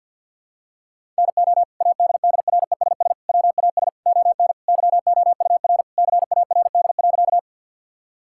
Morse code, heard as a keyed tone, about 52 wpm, 700 Hz. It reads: NO ADDRESS WAS ON YOUR CARD.
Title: Sentences from Top 1000 Words - Review Part 6 - 50wpm